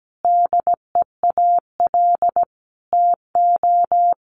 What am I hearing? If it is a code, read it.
Morse code, heard as a keyed tone, about 17 wpm, 700 Hz. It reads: DEAL TO
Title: Sentences from Top 1000 Words 17wpm